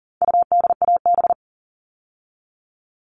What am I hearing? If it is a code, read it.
Morse code, heard as a keyed tone, about 40 wpm, 700 Hz. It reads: UDAB
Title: All Cut Numbers - 4-Digit 40wpm